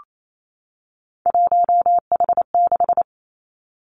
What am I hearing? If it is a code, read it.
Morse code, heard as a keyed tone, about 28 wpm, 700 Hz. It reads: 1H6
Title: Three Challenging Letter-Number Combinations 28wpm